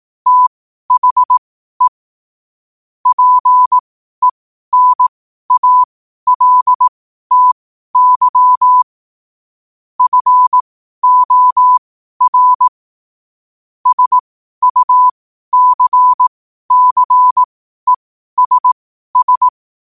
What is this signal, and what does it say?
Morse code, heard as a keyed tone, about 18 wpm, spaced as for 12 wpm, 1000 Hz. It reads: THE PENALTY FOR SUCCESS
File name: qotdfiles / 12wpm